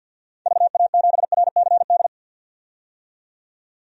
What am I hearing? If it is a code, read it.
Morse code, heard as a keyed tone, about 46 wpm, 700 Hz. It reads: VA7RCD
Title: DXCC-100 Call Signs - 6 Characters - 50wpm